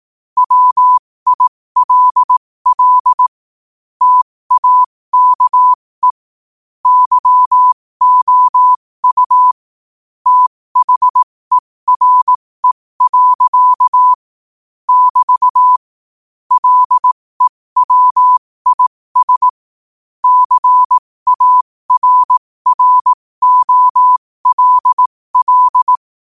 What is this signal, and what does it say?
Morse code, heard as a keyed tone, about 18 wpm, spaced as for 15 wpm, 1000 Hz. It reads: WILL TAKE YOU THERE. <BT> LEWIS CARROLL
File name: qotdfiles / 15wpm